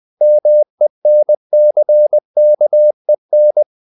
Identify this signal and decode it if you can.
Morse code, heard as a keyed tone, about 20 wpm, 600 Hz. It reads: MENCKEN
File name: CW / Practice / EnglishQuotations0320